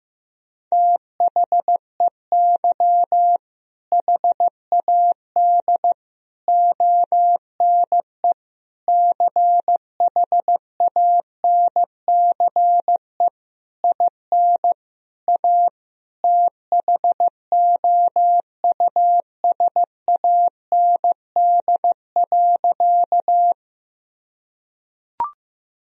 Morse code, heard as a keyed tone, about 15 wpm, 700 Hz. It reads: THEY HAD ONE CHANCE IN A THOUSAND.
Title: Sentences from Top 1000 Words - Review Part 3 - 15wpm